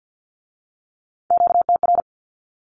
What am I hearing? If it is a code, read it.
Morse code, heard as a keyed tone, about 50 wpm, 700 Hz. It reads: QTR
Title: Q-Codes with Spoken Meaning - 50wpm